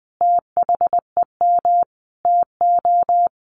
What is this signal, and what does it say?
Morse code, heard as a keyed tone, about 20 wpm, 700 Hz. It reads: THEM TO
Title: Sets of 5 Words Encore 20wpm